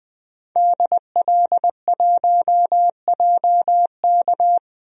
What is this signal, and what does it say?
Morse code, heard as a keyed tone, about 20 wpm, 700 Hz. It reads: DL1JK
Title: DXCC-100 Call Signs - 5 Characters - 20wpm